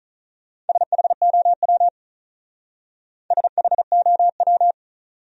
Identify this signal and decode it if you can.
Morse code, heard as a keyed tone, about 46 wpm, 700 Hz. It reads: SHOW SHOW